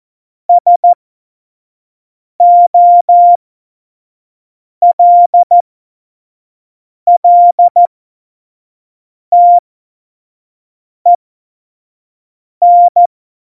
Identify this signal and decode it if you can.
Morse code, heard as a keyed tone, about 14 wpm, spaced as for 5 wpm, 700 Hz. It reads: SOLLTEN
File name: Morsekurs / Witze / Witz 09. Januar 2018, 25 F70